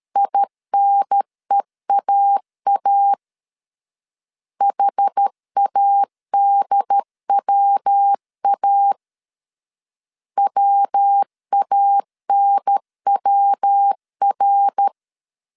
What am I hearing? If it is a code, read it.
Morse code, heard as a keyed tone, about 13 wpm, 800 Hz. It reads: INEAA HADWA WANWR
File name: CW / DET / 1 3 1500C S C 12wpm 2750